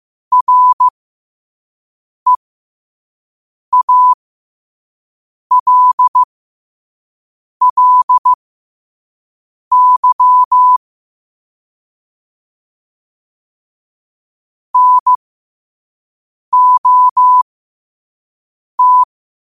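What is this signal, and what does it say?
Morse code, heard as a keyed tone, about 15 wpm, spaced as for 5 wpm, 1000 Hz. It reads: REALLY NOT
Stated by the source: Morse quote-of-the-day practice at 5 wpm.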